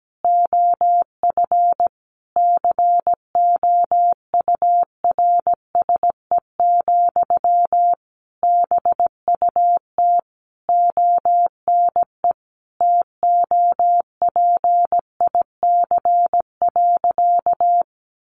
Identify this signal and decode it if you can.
Morse code, heard as a keyed tone, about 17 wpm, 700 Hz. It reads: OF COURSE, BUT ONE TOPIC.
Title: Sentences from Top 1200 Words 17wpm